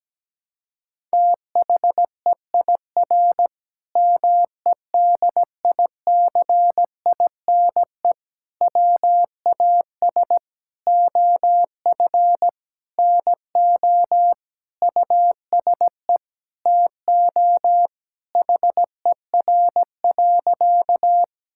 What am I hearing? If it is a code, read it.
Morse code, heard as a keyed tone, about 17 wpm, 700 Hz. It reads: THEIR MEDICINE WAS OF NO USE TO HER.
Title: Sentences from Top 1000 Words - Review Part 6 - 17wpm